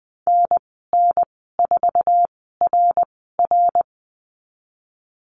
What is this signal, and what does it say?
Morse code, heard as a keyed tone, about 20 wpm, spaced as for 15 wpm, 700 Hz. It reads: NN4RR
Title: 069 Lesson 12g - Letter d - Call Signs 20wpm